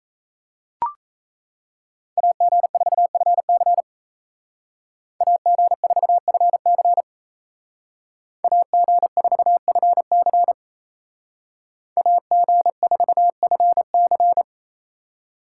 Morse code, heard as a keyed tone, about 34 wpm, 700 Hz. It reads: AG4FC AG4FC AG4FC AG4FC
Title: Speed Racing - USA 2x2 Call Signs 28wpm